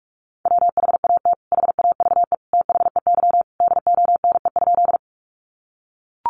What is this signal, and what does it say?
Morse code, heard as a keyed tone, about 44 wpm, 700 Hz. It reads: WHAT HAVE THEY DONE?
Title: Sentences from Top 500 Words 45wpm